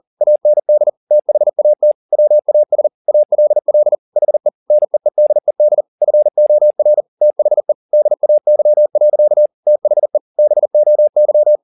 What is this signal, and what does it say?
Morse code, heard as a keyed tone, about 40 wpm, 600 Hz. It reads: AND THAT WAS ALL HE NEEDED FOR THE DAY. THE BOY